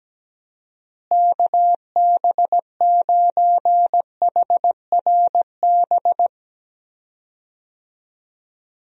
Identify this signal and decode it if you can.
Morse code, heard as a keyed tone, about 17 wpm, 700 Hz. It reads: KB9HRB
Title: USA Call Signs - 6 Characters - Repeated 2x - 17wpm